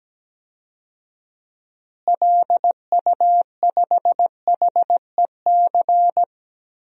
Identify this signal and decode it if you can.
Morse code, heard as a keyed tone, about 17 wpm, 700 Hz. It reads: LU5HEC
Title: DXCC-100 Call Signs - 6 Characters - Repeated 2x - 17wpm